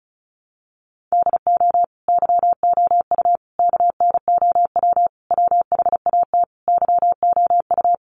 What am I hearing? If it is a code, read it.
Morse code, heard as a keyed tone, about 34 wpm, 700 Hz. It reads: DO YOU KNOW WHAT YOU
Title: Sentences from Top 100 Words 35wpm